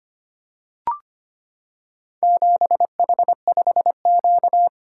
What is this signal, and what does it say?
Morse code, heard as a keyed tone, about 25 wpm, 700 Hz. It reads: 7H5Q
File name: practice / 25wpm / Four Letter-Number Combinations - 25wpm